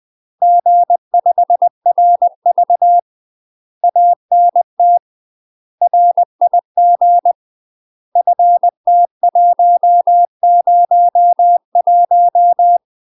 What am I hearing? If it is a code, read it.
Morse code, heard as a keyed tone, about 20 wpm, 700 Hz. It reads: G5RV ANT RIG FT101
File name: UK MASTER QSOs TRAINING / 20WPM / UK MASTER QSOs 0015